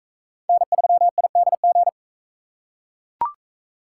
Morse code, heard as a keyed tone, about 41 wpm, 700 Hz. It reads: N3IDG